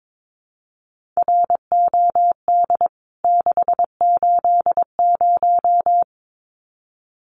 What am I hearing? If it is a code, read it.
Morse code, heard as a keyed tone, about 22 wpm, 700 Hz. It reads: ROD 680
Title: CWT Exchange 22wpm